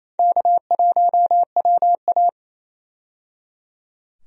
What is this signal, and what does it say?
Morse code, heard as a keyed tone, about 28 wpm, 700 Hz. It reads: K1WA
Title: USA Call Signs - 4 Characters - 28wpm